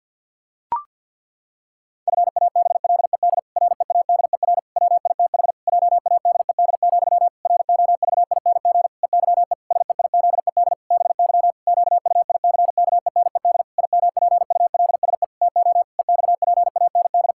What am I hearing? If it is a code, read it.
Morse code, heard as a keyed tone, about 50 wpm, 700 Hz. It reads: FABLED READER WITH JADED, ROVING EYE SEIZED BY QUICKENED IMPULSE TO EXPAND
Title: Pangram Challenge v2 - 50wpm